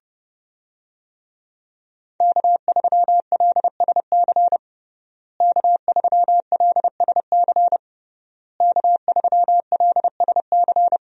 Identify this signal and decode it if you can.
Morse code, heard as a keyed tone, about 32 wpm, 700 Hz. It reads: K3LSC K3LSC K3LSC
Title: USA Call Signs - 5 Characters - Repeated 3x - 30wpm